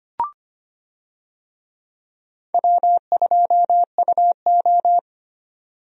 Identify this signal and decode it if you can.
Morse code, heard as a keyed tone, about 25 wpm, 700 Hz. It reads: W2UO
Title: USA Call Signs - 4 Characters - 25wpm